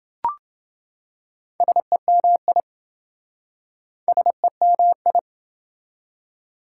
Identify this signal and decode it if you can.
Morse code, heard as a keyed tone, about 28 wpm, 700 Hz. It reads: SEMI SEMI